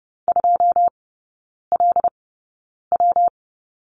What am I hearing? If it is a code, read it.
Morse code, heard as a keyed tone, about 30 wpm, spaced as for 9 wpm, 700 Hz. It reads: 2LW
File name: Lesson 19e - Letter w - Three Characters 30wpm